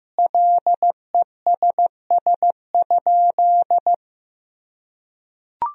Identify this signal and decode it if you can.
Morse code, heard as a keyed tone, about 16 wpm, 700 Hz. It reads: LESS?
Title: Sentences from Top 1000 Words - Review Part 2 - 15wpm